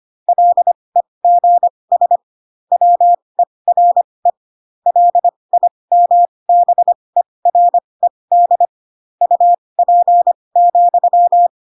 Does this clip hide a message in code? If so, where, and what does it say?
Yes: Morse code from 0.3 to 11.5 s, heard as a keyed tone, about 25 wpm, spaced as for 20 wpm, 700 Hz. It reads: LEGS WERE LIMBERED UP,